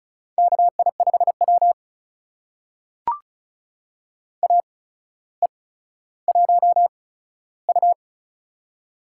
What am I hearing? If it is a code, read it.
Morse code, heard as a keyed tone, about 36 wpm, spaced as for 15 wpm, 700 Hz. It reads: KI5W AE1U
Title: USA Call Signs - 4 Characters - 8x Character Spacing - 35wpm